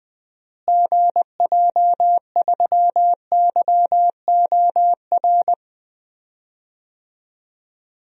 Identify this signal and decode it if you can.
Morse code, heard as a keyed tone, about 20 wpm, 700 Hz. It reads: GJ3YOR
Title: DXCC-100 Call Signs - 6 Characters - Repeated 3x - 20wpm